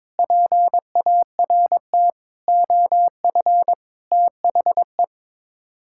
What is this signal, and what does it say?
Morse code, heard as a keyed tone, about 22 wpm, 700 Hz. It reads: PART OF THE
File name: practice / 22wpm / Sets of 6 Words - 22wpm